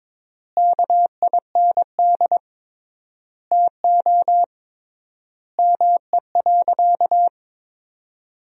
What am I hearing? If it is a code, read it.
Morse code, heard as a keyed tone, about 22 wpm, 700 Hz. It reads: KIND TO ME.
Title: 3x Word-Spacing - Sentences from Top 200 Words - 22wpm